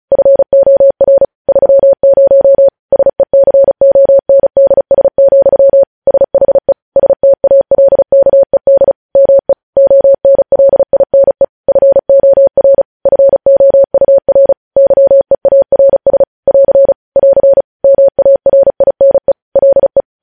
Morse code, heard as a keyed tone, about 36 wpm, 550 Hz. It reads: FOR 30 SECONDS, SHE STALKED ME ONLINE FOR FOUR YEARS <AR> <AR> MARINE LE
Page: News headlines in Morse code at 35 wpm